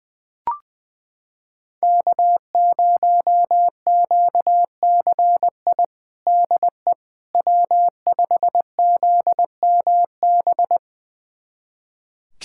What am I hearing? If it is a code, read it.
Morse code, heard as a keyed tone, about 20 wpm, 700 Hz. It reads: K0QCI DE W5ZMB
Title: USA Callsign DE Callsign - 5 Characters - 20wpm